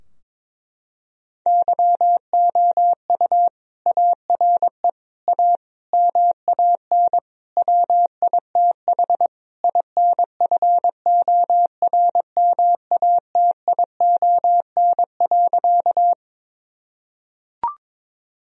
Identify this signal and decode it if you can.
Morse code, heard as a keyed tone, about 21 wpm, 700 Hz. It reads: YOU ARE A MAN WITH INFORMATION.
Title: Sentences from Top 1000 Words - Review Part 3 - 22wpm